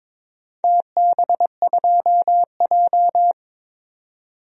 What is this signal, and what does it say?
Morse code, heard as a keyed tone, about 22 wpm, 700 Hz. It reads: TB2J
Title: DXCC-100 Call Signs - 4 Characters - 22wpm